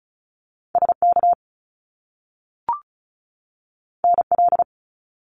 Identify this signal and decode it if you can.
Morse code, heard as a keyed tone, about 36 wpm, 700 Hz. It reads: SK NL
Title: Canadian Provinces and Territories 35wpm